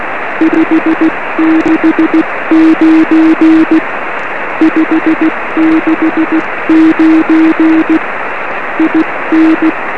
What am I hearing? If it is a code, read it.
Morse code, heard as a keyed tone, about 16 wpm, 320 Hz, with noise. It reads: 569 569 IN
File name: 1911z dj6ff 13cm